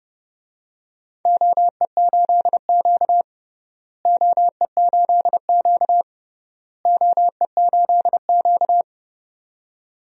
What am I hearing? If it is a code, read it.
Morse code, heard as a keyed tone, about 30 wpm, 700 Hz. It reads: OE8Q OE8Q OE8Q
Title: DXCC-100 Call Signs - 4 Characters - Repeated 3x - 30wpm